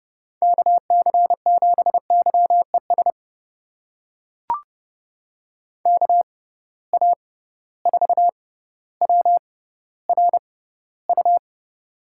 Morse code, heard as a keyed tone, about 30 wpm, spaced as for 15 wpm, 700 Hz. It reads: KC7YES KA4WRU
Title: USA Call Signs - 6 Characters - 6x Character Spacing - 30wpm